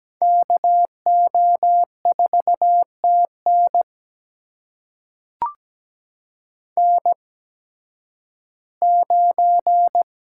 Morse code, heard as a keyed tone, about 17 wpm, spaced as for 11 wpm, 700 Hz. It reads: KO4TN N9
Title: USA Call Signs - 5 Characters - 8x Character Spacing - 17wpm